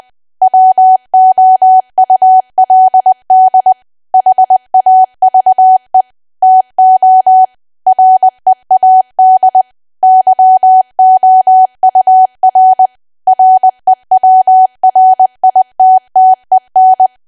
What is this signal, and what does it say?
Morse code, heard as a keyed tone, about 20 wpm, 750 Hz. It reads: WOULD HAVE TO READ YOUR REWRITTEN
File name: cwa / W1AW-Level3 / 20WPM